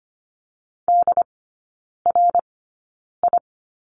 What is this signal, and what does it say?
Morse code, heard as a keyed tone, about 25 wpm, spaced as for 9 wpm, 700 Hz. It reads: DRI
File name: course / 25:10wpm / 093 Lesson 16e - Letter c - Three Characters 25wpm